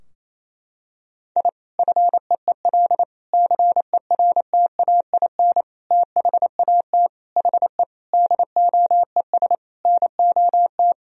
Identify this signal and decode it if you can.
Morse code, heard as a keyed tone, about 27 wpm, 700 Hz. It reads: I FEEL CERTAIN THAT HE DOES NOT
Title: Sentences from Top 1000 Words - Review Part 2 - 28wpm